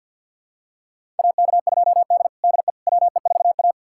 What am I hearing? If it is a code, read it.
Morse code, heard as a keyed tone, about 52 wpm, 700 Hz. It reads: AK2D DE WE4A
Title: USA Callsign DE Callsign - 4 Characters - 50wpm